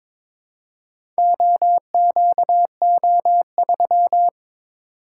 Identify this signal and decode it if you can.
Morse code, heard as a keyed tone, about 22 wpm, 700 Hz. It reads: OQO3
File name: Four Letter-Number Combinations - 22wpm